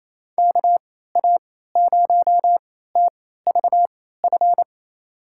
Morse code, heard as a keyed tone, about 28 wpm, spaced as for 16 wpm, 700 Hz. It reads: KA0TVF
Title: USA Call Signs - 3x Character Spacing - 28wpm